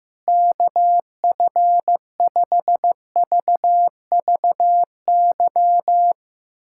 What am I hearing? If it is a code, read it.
Morse code, heard as a keyed tone, about 15 wpm, 700 Hz. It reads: KF5VVY
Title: USA Callsign DE Callsign - 6 Characters - 15wpm